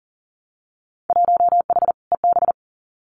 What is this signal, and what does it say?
Morse code, heard as a keyed tone, about 40 wpm, 700 Hz. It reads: 1H EB
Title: Winter Field Day Exchanges 40wpm